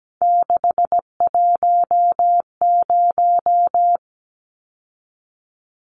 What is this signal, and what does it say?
Morse code, heard as a keyed tone, about 17 wpm, 700 Hz. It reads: 610
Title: Numbers - 3-Digit 17wpm